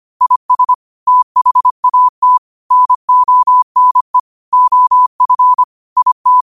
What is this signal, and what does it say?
Morse code, heard as a keyed tone, about 25 wpm, 1000 Hz. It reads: IS THAT NONE OF IT
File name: qotdfiles / 25wpm